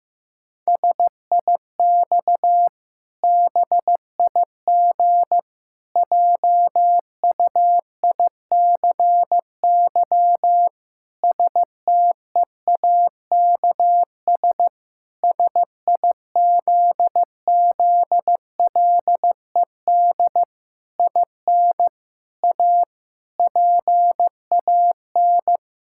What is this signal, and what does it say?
Morse code, heard as a keyed tone, about 15 wpm, 700 Hz. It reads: SIX BIG JUICY STEAKS SIZZLED IN A PAN
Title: Pangram Challenge v2 - 15wpm